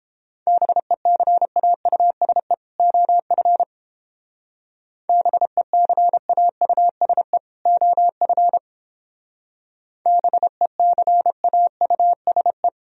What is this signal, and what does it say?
Morse code, heard as a keyed tone, about 29 wpm, 700 Hz. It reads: BECAUSE OF BECAUSE OF BECAUSE